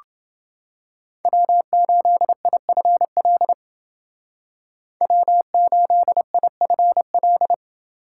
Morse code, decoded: W8IFL W8IFL

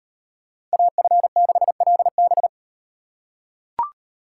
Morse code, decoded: AF6LB

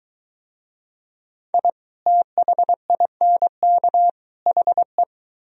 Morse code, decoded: I THINK HE

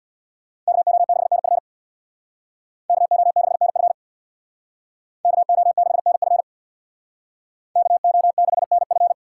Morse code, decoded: KO6NF KO6NF KO6NF KO6NF

